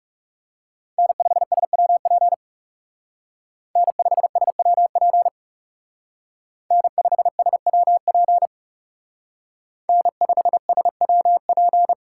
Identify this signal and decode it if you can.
Morse code, heard as a keyed tone, about 36 wpm, 700 Hz. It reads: N5SWP N5SWP N5SWP N5SWP
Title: Speed Racing - USA 1x3 Call Signs 30wpm